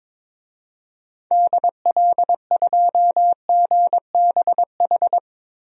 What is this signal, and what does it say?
Morse code, heard as a keyed tone, about 21 wpm, 700 Hz. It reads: DL2GBH
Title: DXCC-100 Call Signs - 6 Characters - Repeated 3x - 22wpm